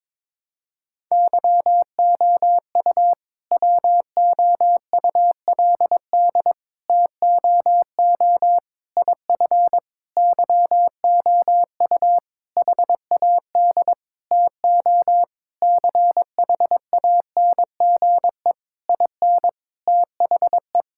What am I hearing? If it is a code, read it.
Morse code, heard as a keyed tone, about 22 wpm, 690 Hz. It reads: YOU WOULD TOO IF YOU HAD TO CHANGE IN THE
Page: morse-code-ninja-practice-22wpm